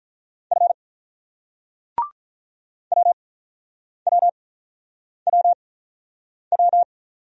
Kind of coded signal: Morse code

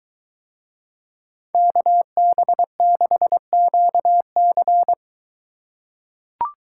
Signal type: Morse code

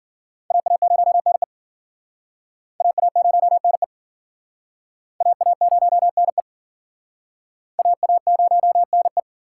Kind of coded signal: Morse code

